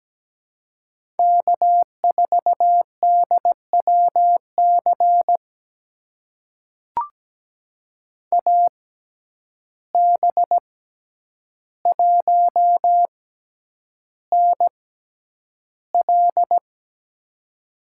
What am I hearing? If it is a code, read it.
Morse code, heard as a keyed tone, about 17 wpm, spaced as for 9 wpm, 700 Hz. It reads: K4DWC AB1NL
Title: USA Call Signs - 6x Character Spacing - 17wpm